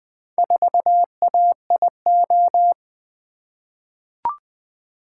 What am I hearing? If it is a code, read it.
Morse code, heard as a keyed tone, about 20 wpm, 700 Hz. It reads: 4AIO
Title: Four Letter-Number Combinations - 20wpm